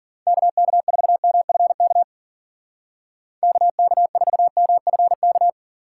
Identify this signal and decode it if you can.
Morse code, heard as a keyed tone, about 51 wpm, 700 Hz. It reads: KK4MFK KK4MFK